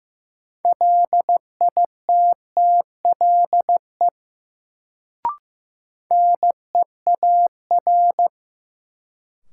Morse code, heard as a keyed tone, about 15 wpm, 700 Hz. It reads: LITTLE NEAR
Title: Sentences from Top 200 Words - New Word Drill - 15wpm